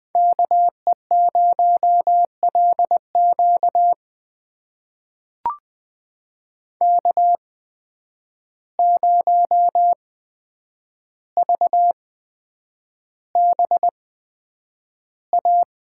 Morse code, heard as a keyed tone, about 20 wpm, spaced as for 9 wpm, 700 Hz. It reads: KE0LQ K0VBA
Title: USA Call Signs - 5 Characters - 8x Character Spacing - 20wpm